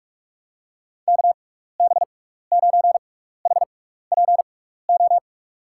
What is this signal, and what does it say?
Morse code, heard as a keyed tone, about 47 wpm, spaced as for 16 wpm, 700 Hz. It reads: KB9HPO
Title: USA Call Signs - 6x Character Spacing - 45wpm